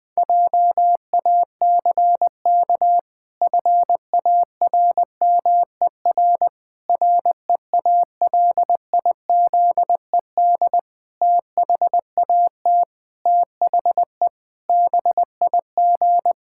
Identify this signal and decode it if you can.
Morse code, heard as a keyed tone, about 20 wpm, 700 Hz. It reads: JACK FARMER REALIZED THAT THE BIG
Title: Pangram Challenge v2 - 20wpm